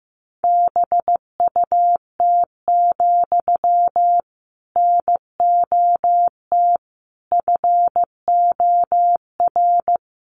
Morse code, heard as a keyed tone, about 15 wpm, 700 Hz. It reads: BUT, NOT FOR
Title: Sentences from Top 600 Words 15wpm